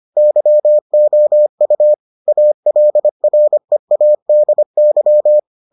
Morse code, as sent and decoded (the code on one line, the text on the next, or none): -.-- --- ..- / .- .-.. .-. .- -.. -.--
YOU ALREADY